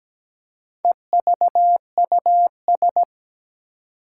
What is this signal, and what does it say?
Morse code, heard as a keyed tone, about 17 wpm, 700 Hz. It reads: EVUS
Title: Four Challenging Letter-Number Combinations 17wpm